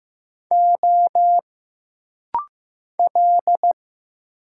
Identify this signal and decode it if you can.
Morse code, heard as a keyed tone, about 15 wpm, 700 Hz. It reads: O L
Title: Single Letters - Rapid-Fire 15wpm